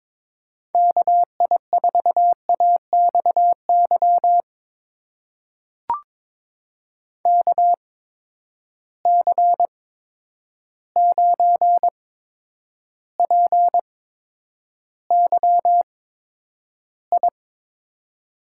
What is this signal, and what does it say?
Morse code, heard as a keyed tone, about 22 wpm, spaced as for 9 wpm, 700 Hz. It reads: KI4AXY KC9PYI
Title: USA Call Signs - 6 Characters - 8x Character Spacing - 22wpm